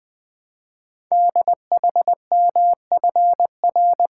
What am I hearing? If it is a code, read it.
Morse code, heard as a keyed tone, about 20 wpm, 700 Hz. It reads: DHMFR
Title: Code Groups - 5 Letters - 20wpm